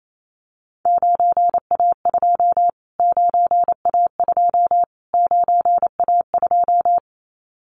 Morse code, heard as a keyed tone, about 28 wpm, 700 Hz. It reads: 9A2 9A2 9A2
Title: Callsign Prefixes for Top 25 DXCC - Repeated 3x - 28wpm